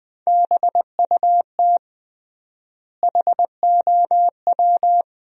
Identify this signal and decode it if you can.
Morse code, heard as a keyed tone, about 20 wpm, 700 Hz. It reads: BUT HOW
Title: 3x Word-Spacing - Sentences from Top 100 Words - 20wpm